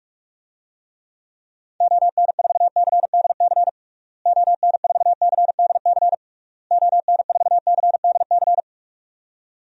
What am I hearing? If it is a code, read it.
Morse code, heard as a keyed tone, about 46 wpm, 700 Hz. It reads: ON4CDC ON4CDC ON4CDC
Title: DXCC-100 Call Signs - 6 Characters - Repeated 3x - 45wpm